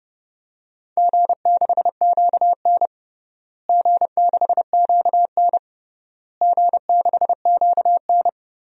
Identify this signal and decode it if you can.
Morse code, heard as a keyed tone, about 30 wpm, 700 Hz. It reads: G6QN G6QN G6QN
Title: DXCC-100 Call Signs - 4 Characters - Repeated 3x - 30wpm